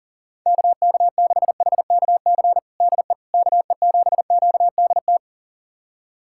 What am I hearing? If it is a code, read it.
Morse code, heard as a keyed tone, about 40 wpm, 700 Hz. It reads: KK6HKC DE KE7QDT
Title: USA Callsign DE Callsign - 6 Characters - 40wpm